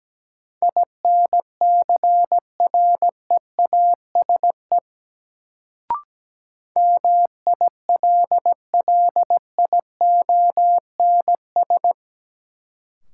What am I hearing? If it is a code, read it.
Morse code, heard as a keyed tone, about 17 wpm, 700 Hz. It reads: INCREASE MILLIONS
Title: Sets of 5 Words - New Word Drill - 17wpm